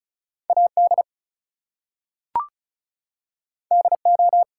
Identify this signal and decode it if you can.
Morse code, heard as a keyed tone, about 36 wpm, 700 Hz. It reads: AD DO
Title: Two Letter Words - 35wpm